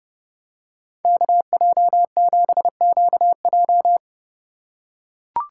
Morse code, heard as a keyed tone, about 30 wpm, 700 Hz. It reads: KJ7QJ